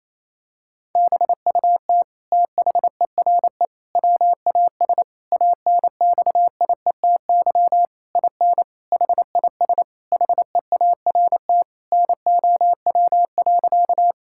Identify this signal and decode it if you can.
Morse code, heard as a keyed tone, about 27 wpm, 700 Hz. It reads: BUT THERE WAS ANXIETY IN HIS HEART NOW.
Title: Sentences from Top 1600 Words 28wpm